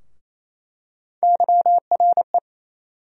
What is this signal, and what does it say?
Morse code, heard as a keyed tone, about 28 wpm, 700 Hz. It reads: YRE
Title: USA Call Sign Postfixes - 3 Letters 28wpm